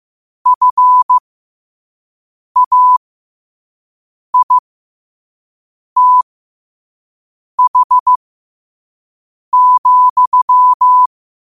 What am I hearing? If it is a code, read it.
Morse code, heard as a keyed tone, about 15 wpm, spaced as for 5 wpm, 1000 Hz. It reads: FAITH,
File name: qotdfiles / 5wpm